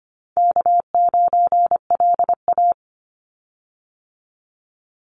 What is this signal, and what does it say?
Morse code, heard as a keyed tone, about 25 wpm, 700 Hz. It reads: K9LA
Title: Easy Call Signs - 25wpm